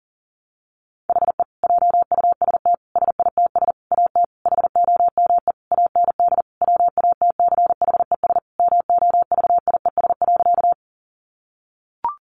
Morse code, heard as a keyed tone, about 40 wpm, 700 Hz. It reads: HE JUST SITS AT HOME AND WATCHES MOVIES.
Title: Additional Easy Phrases 40wpm